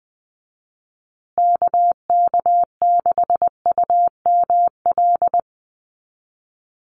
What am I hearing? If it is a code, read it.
Morse code, heard as a keyed tone, about 20 wpm, 700 Hz. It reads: KK6UML
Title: USA Call Signs Repeated 3x 20wpm